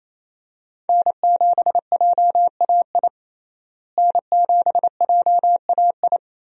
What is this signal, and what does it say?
Morse code, heard as a keyed tone, about 28 wpm, 700 Hz. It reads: N7JAI N7JAI
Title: USA Call Signs - 5 Characters - Repeated 3x - 28wpm